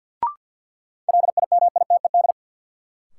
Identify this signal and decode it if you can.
Morse code, heard as a keyed tone, about 48 wpm, 700 Hz. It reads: LIMITED
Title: Sentences from Top 900 Words - New Word Drill - 50wpm